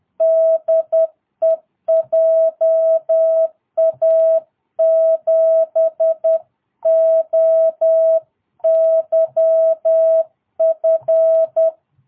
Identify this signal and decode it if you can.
Morse code, heard as a keyed tone, about 10 wpm, 650 Hz. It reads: DE JA7OYF
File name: jh8cbh / 2022cw